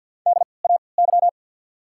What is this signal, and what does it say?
Morse code, heard as a keyed tone, about 48 wpm, spaced as for 28 wpm, 690 Hz. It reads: DAY
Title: 3x Character Spacing - Top 100 Words - 50wpm